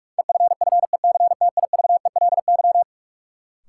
Morse code, heard as a keyed tone, about 46 wpm, 700 Hz. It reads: EFFECTIVELY
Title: Sentences from Top 1400 Words - New Word Drill - 45wpm